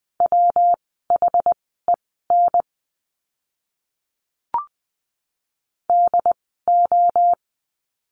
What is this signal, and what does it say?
Morse code, heard as a keyed tone, about 20 wpm, spaced as for 14 wpm, 700 Hz. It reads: WHEN DO
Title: 115 Lesson 19f - Letter w - Words 20wpm